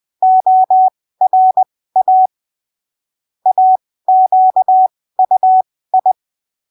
Morse code, heard as a keyed tone, about 20 wpm, spaced as for 15 wpm, 750 Hz. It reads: ORA AQUI
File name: arquivos / texto / texto 26 20ppm